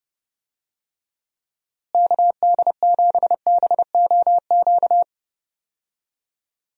Morse code, heard as a keyed tone, about 32 wpm, 700 Hz. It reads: KD7BOQ
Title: USA Call Signs - 6 Characters - Repeated 3x - 30wpm